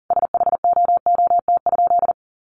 Morse code, heard as a keyed tone, about 40 wpm, 700 Hz. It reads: SHOOT?
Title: A Sound Of Thunder 40wpm